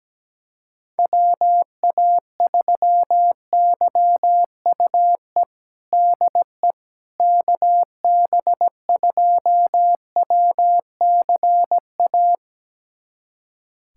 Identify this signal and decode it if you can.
Morse code, heard as a keyed tone, about 17 wpm, 700 Hz. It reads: WA3YUE DE KB2WCA
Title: USA Callsign DE Callsign 17wpm